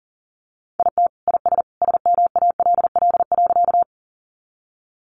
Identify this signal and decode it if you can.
Morse code, heard as a keyed tone, about 40 wpm, 700 Hz. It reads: IT IS SMALL.
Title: Easy Phrases 40wpm